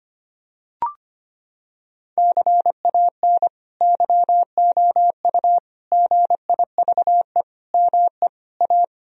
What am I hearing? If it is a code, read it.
Morse code, heard as a keyed tone, about 25 wpm, 700 Hz. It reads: CAN YOU GIVE ME A
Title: Sentences from Top 1000 Words - Review Part 2 - 25wpm